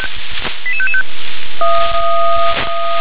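Five-tone selective call (ZVEI), heard as quick runs of five tones, tones of about 70 ms. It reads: 88555
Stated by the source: ZVEI five-tone call 88865, received on SDR.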